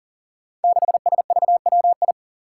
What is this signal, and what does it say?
Morse code, heard as a keyed tone, about 40 wpm, 700 Hz. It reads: 6SVWI
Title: Five Challenging Letter-Number Combinations 40wpm